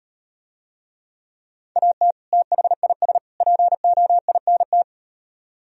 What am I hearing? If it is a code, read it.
Morse code, heard as a keyed tone, about 38 wpm, 700 Hz. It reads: AT THIS POINT